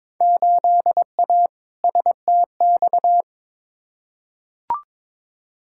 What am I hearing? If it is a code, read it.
Morse code, heard as a keyed tone, about 22 wpm, 700 Hz. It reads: 8A STX
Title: Field Day Exchanges Encore - 22wpm